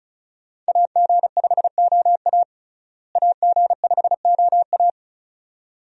Morse code, heard as a keyed tone, about 36 wpm, 700 Hz. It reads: AG5OA AG5OA
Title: USA Call Signs - 5 Characters - Repeated 2x - 35wpm